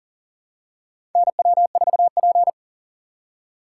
Morse code, heard as a keyed tone, about 37 wpm, 700 Hz. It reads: NW4P
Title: USA Call Signs - 4 Characters - 3x Character Spacing - 40wpm